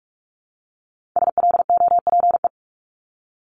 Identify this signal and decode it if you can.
Morse code, heard as a keyed tone, about 46 wpm, 700 Hz. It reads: SLOPE
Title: Multi-letter Phonemes Repeated 3x 45wpm